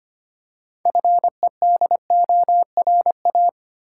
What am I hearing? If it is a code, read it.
Morse code, heard as a keyed tone, about 25 wpm, 700 Hz. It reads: FEDORA
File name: practice / 25wpm / Difficult Words with Hints - 25wpm